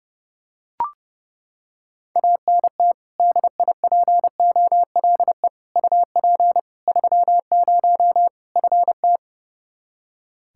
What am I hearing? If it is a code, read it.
Morse code, heard as a keyed tone, about 30 wpm, 700 Hz. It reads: ANT DIPOLE UP 30 FT
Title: K1USN SST QSO Practice 30wpm